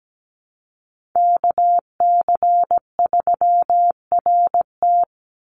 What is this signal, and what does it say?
Morse code, heard as a keyed tone, about 17 wpm, 700 Hz. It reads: KC3RT
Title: NA SOTA Chaser Callsigns 17wpm